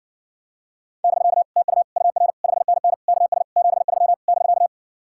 Morse code, heard as a keyed tone, about 60 wpm, 700 Hz. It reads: <BK> TU UR 5NN KS 73 <BK>